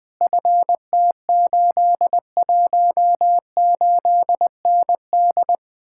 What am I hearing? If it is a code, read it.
Morse code, heard as a keyed tone, about 20 wpm, 700 Hz. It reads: FT818ND
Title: Speed-Racing Common QSO Phrases - 17wpm